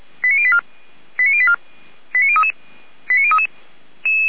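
Five-tone selective call (ZVEI), heard as quick runs of five tones, tones of about 70 ms. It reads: 89084 89084 89033 89033
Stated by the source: ZVEI five-tone selective call received via SDR.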